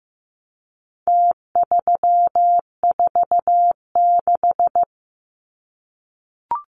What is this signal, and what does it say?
Morse code, heard as a keyed tone, about 15 wpm, 700 Hz. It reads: T346
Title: Numbers - Contest Style - 4-Digit 15wpm